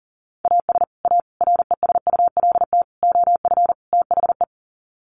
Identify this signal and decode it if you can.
Morse code, heard as a keyed tone, about 40 wpm, 700 Hz. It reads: AS A RESULT OF THE
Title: Sets of 5 Words 40wpm